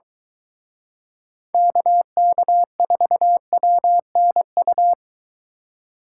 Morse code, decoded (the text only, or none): KK4WNU